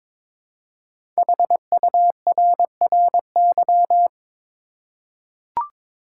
Morse code, heard as a keyed tone, about 22 wpm, 700 Hz. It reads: HURRY
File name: 5 Letter Words - First 3 Letters Given - 22wpm